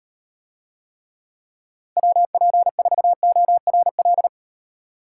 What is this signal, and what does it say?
Morse code, heard as a keyed tone, about 38 wpm, 700 Hz. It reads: WP4ORL